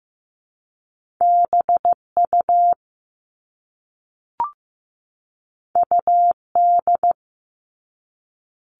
Morse code, heard as a keyed tone, about 15 wpm, 700 Hz. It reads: BU UD
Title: All Cut Numbers - 2-Digit 15wpm